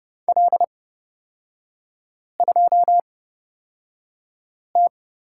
Morse code, heard as a keyed tone, about 30 wpm, spaced as for 5 wpm, 700 Hz. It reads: L2T
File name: Lesson 36e - BT - Three Characters 30wpm